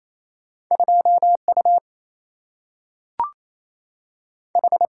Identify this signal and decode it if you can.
Morse code, heard as a keyed tone, about 28 wpm, 700 Hz. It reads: 2 U H
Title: Single-Letter Single-Number 28wpm